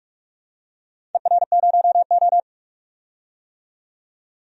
Morse code, heard as a keyed tone, about 42 wpm, 700 Hz. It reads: ER0O
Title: DXCC-100 Call Signs - 4 Characters - 45wpm